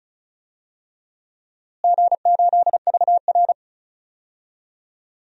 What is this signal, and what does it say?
Morse code, heard as a keyed tone, about 36 wpm, 700 Hz. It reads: G8VR
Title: DXCC-100 Call Signs - 4 Characters - Repeated 3x - 35wpm